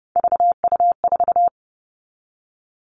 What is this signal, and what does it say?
Morse code, heard as a keyed tone, about 30 wpm, 700 Hz. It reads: VU4
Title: All Cut Numbers - 3-Digit 30wpm